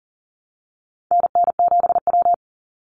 Morse code, heard as a keyed tone, about 40 wpm, 700 Hz. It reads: NN7W
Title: Call Signs Repeated 3x - 40wpm